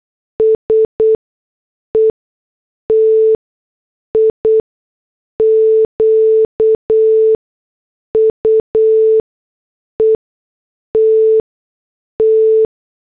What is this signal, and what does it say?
Morse code, heard as a keyed tone, about 8 wpm, spaced as for 6 wpm, 440 Hz. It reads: S ETIQUETT